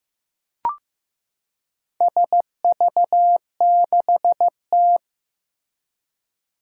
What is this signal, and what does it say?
Morse code, heard as a keyed tone, about 15 wpm, 700 Hz. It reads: SV6T
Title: Four Challenging Letter-Number Combinations 15wpm